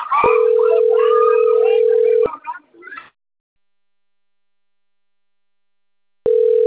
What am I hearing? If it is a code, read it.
Ringback tone (North America), 440 + 480 Hz.
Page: Atibaia 1 CP Cartao